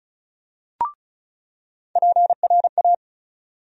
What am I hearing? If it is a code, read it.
Morse code, heard as a keyed tone, about 36 wpm, 700 Hz. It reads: PRA
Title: USA Call Sign Postfixes - 3 Letters 35wpm